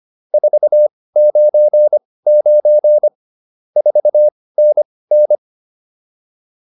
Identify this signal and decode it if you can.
Morse code, heard as a keyed tone, about 25 wpm, spaced as for 18 wpm, 600 Hz. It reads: 499 4NN